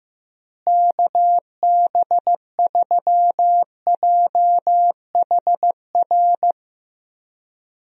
Morse code, heard as a keyed tone, about 15 wpm, 700 Hz. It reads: KB3JHR
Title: USA Call Signs - 6 Characters - Repeated 2x - 15wpm